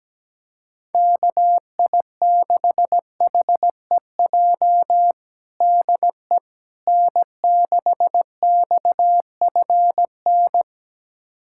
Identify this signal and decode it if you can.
Morse code, heard as a keyed tone, about 17 wpm, 700 Hz. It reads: KI6HEJ DE N6XFN
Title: USA Callsign DE Callsign 17wpm